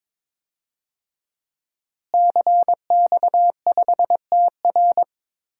Morse code, heard as a keyed tone, about 21 wpm, 700 Hz. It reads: CX5TR
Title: DXCC-100 Call Signs - 5 Characters - Repeated 3x - 22wpm